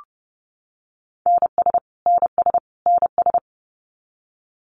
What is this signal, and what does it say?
Morse code, heard as a keyed tone, about 30 wpm, 700 Hz. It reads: NS NS NS
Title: Canadian Provinces and Territories - Repeated 3x - 30wpm